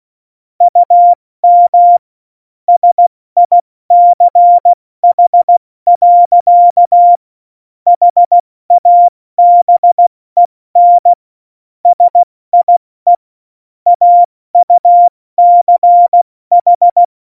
Morse code, heard as a keyed tone, about 16 wpm, 700 Hz. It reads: UM SICH. HABEN SIE AUCH
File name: Morsekurs / Witze / Witz 27. Februar 2018, 70 F80